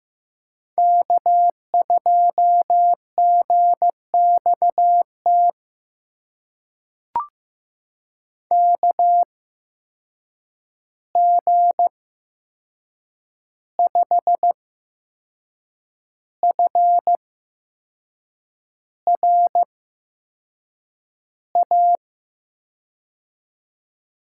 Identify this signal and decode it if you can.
Morse code, heard as a keyed tone, about 15 wpm, spaced as for 6 wpm, 700 Hz. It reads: K2GXT KG5FRA
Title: USA Call Signs - 8x Character Spacing - 15wpm